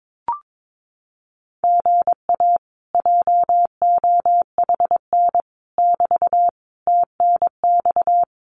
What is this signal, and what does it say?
Morse code, heard as a keyed tone, about 22 wpm, 700 Hz. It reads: GA JOHN <BT> TNX
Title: Common QSO Phrases Encore 22wpm